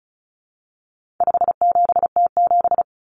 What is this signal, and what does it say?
Morse code, heard as a keyed tone, about 34 wpm, 700 Hz. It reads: 57T7
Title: Common Cut Numbers - 4-Digit 35wpm